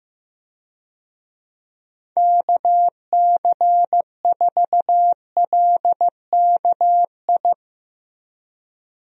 Morse code, heard as a keyed tone, about 15 wpm, 700 Hz. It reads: KC4LKI
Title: USA Call Signs - 6 Characters - Repeated 3x - 15wpm